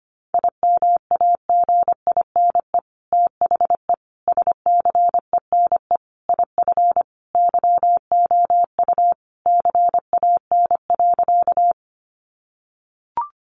Morse code, heard as a keyed tone, about 25 wpm, 700 Hz. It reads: IMAGINE THE SCENE IF YOU CAN.
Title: Sentences from Top 1400 Words 25wpm